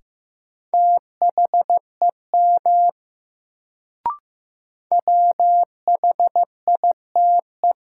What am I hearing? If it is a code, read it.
Morse code, heard as a keyed tone, about 15 wpm, 700 Hz. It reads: THEM WHITE
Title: Sets of 2 Words - New Word Drill - 15wpm